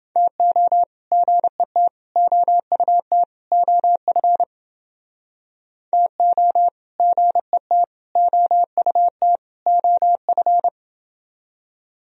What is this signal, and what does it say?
Morse code, heard as a keyed tone, about 29 wpm, 700 Hz. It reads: TO GET OUT OF TO GET OUT OF